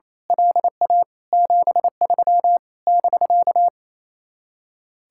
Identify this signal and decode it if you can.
Morse code, heard as a keyed tone, about 26 wpm, 700 Hz. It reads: LA 73 <BK>
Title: Speed-Racing Common QSO Phrases Encore - 28wpm